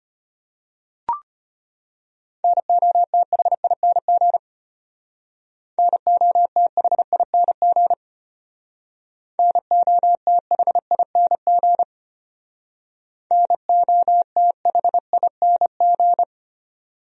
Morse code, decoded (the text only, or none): NOTHING NOTHING NOTHING NOTHING